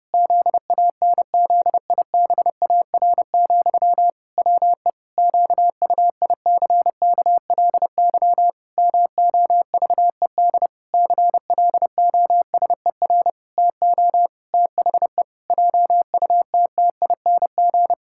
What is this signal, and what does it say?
Morse code, heard as a keyed tone, about 30 wpm, 700 Hz. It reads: ZANZIBAR, WE QUICKLY MOVED CLOSER TO THE JUTTING
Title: Pangram Challenge v2 - 30wpm